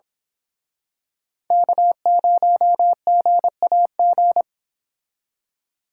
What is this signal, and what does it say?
Morse code, heard as a keyed tone, about 27 wpm, 700 Hz. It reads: K0GAG
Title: Speed Racing - USA 1x3 Call Signs 22wpm